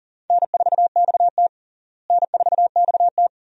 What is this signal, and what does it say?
Morse code, heard as a keyed tone, about 40 wpm, 700 Hz. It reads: N4XT N4XT
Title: USA Call Signs - 4 Characters - Repeated 2x - 40wpm